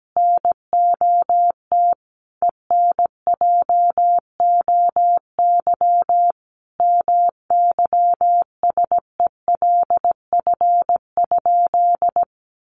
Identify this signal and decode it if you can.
Morse code, heard as a keyed tone, about 17 wpm, 700 Hz. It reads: NOT ENJOY MYSELF?
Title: Sentences from Top 1300 Words 17wpm